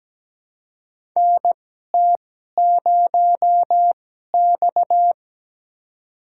Morse code, heard as a keyed tone, about 17 wpm, spaced as for 12 wpm, 700 Hz. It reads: NT0X
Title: USA Call Signs - 4 Characters - 2x Character Spacing - 17wpm